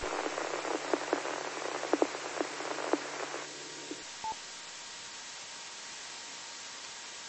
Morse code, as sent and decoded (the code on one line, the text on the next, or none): none